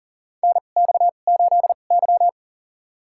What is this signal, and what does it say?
Morse code, heard as a keyed tone, about 40 wpm, spaced as for 29 wpm, 700 Hz. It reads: NX8Y
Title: USA Call Signs - 4 Characters - 2x Character Spacing - 40wpm